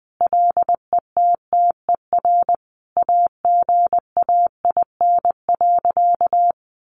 Morse code, heard as a keyed tone, about 20 wpm, 700 Hz. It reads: LETTER AGAIN.
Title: Sentences from Top 300 Words 20wpm